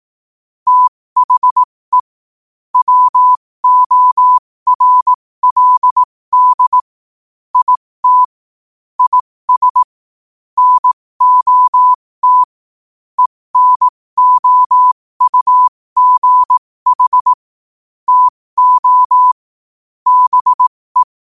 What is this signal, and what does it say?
Morse code, heard as a keyed tone, about 18 wpm, spaced as for 15 wpm, 1000 Hz. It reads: THE WORLD IT IS NOT ENOUGH TO BE